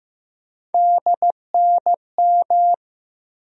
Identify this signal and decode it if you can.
Morse code, heard as a keyed tone, about 15 wpm, 700 Hz. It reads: DNM
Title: USA Call Sign Postfixes - 3 Letters 15wpm